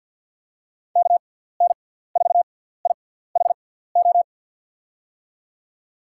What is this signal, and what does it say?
Morse code, heard as a keyed tone, about 51 wpm, spaced as for 17 wpm, 700 Hz. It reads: KN4IHO
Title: USA Call Signs - 6x Character Spacing - 50wpm